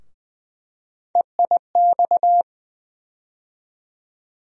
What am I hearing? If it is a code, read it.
Morse code, heard as a keyed tone, about 20 wpm, 700 Hz. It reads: EIX
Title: USA Call Sign Postfixes - 3 Letters 20wpm